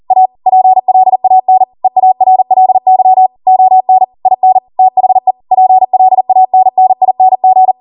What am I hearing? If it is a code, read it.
Morse code, heard as a keyed tone, about 40 wpm, 750 Hz. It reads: A PLAN EARLY ON IN THE PLANNING